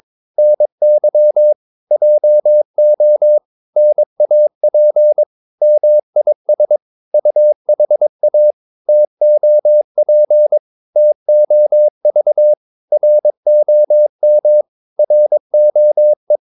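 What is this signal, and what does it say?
Morse code, heard as a keyed tone, about 22 wpm, 600 Hz. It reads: NY JO NAP MIS UHA TOP TOV ROM ROE
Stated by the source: Morse practice of short Danish words at 22 wpm, from CW med Gnister.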